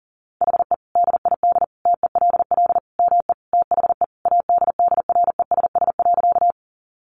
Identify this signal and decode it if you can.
Morse code, heard as a keyed tone, about 40 wpm, 700 Hz. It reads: HE DID TELL ME THE ADDRESS.
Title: Sentences from Top 1000 Words 40wpm